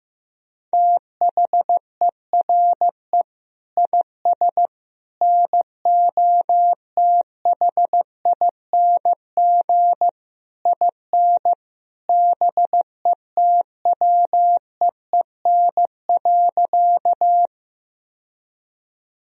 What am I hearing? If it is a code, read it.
Morse code, heard as a keyed tone, about 16 wpm, 700 Hz. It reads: THERE IS NOTHING IN BETWEEN.
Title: Sentences from Top 1000 Words - Review Part 1 - 15wpm